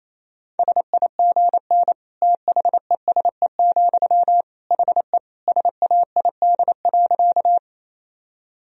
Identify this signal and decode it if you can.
Morse code, heard as a keyed tone, about 28 wpm, 700 Hz. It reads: SIGN THESE, HE SAID.
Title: Sentences from Top 1000 Words - Review Part 1 - 28wpm